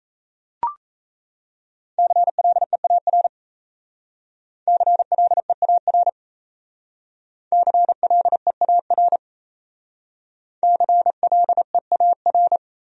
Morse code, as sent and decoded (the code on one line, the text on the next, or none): -.-. .-.. .- .-. / -.-. .-.. .- .-. / -.-. .-.. .- .-. / -.-. .-.. .- .-.
CLEAR CLEAR CLEAR CLEAR